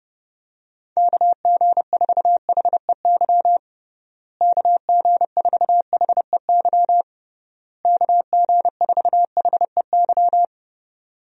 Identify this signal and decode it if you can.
Morse code, heard as a keyed tone, about 30 wpm, 700 Hz. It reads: KG4HEY KG4HEY KG4HEY
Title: USA Call Signs - 6 Characters - Repeated 3x - 30wpm